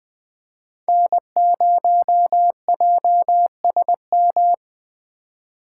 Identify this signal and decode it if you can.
Morse code, heard as a keyed tone, about 21 wpm, 700 Hz. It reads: N0JSM